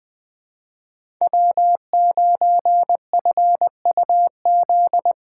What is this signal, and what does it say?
Morse code, heard as a keyed tone, about 20 wpm, 700 Hz. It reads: W9FUZ
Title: USA Call Signs - 6x Character Spacing - 20wpm